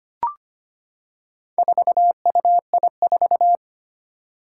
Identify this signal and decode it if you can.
Morse code, heard as a keyed tone, about 25 wpm, 700 Hz. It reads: 4UI4
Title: Four Challenging Letter-Number Combinations 25wpm